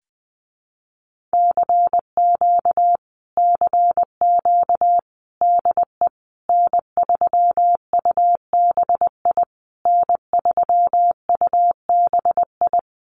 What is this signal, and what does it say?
Morse code, heard as a keyed tone, about 20 wpm, 700 Hz. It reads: CQ CQ DE N3UBI N3UBI
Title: Common QSO Phrases Encore 20wpm